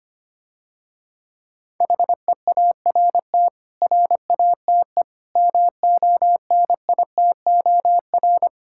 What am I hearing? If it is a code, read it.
Morse code, heard as a keyed tone, about 25 wpm, 700 Hz. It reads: HEART RATE MONITOR
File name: practice / 25wpm / Difficult Words - Speed-Racing with Hints - 25wpm